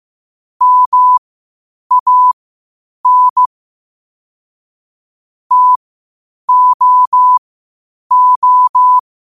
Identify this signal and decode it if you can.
Morse code, heard as a keyed tone, about 15 wpm, spaced as for 8 wpm, 1000 Hz. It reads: MAN TOO
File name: qotdfiles / 8wpm